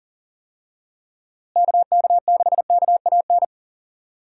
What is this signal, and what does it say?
Morse code, heard as a keyed tone, about 40 wpm, 700 Hz. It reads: KK6KAN
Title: USA Call Signs - 6 Characters - Repeated 2x - 40wpm